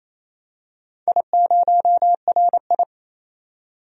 Morse code, heard as a keyed tone, about 27 wpm, 700 Hz. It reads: I0RI
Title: DXCC-100 Call Signs - 4 Characters - 28wpm